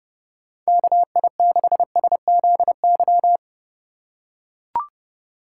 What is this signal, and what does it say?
Morse code, heard as a keyed tone, about 28 wpm, 700 Hz. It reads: KI6SZY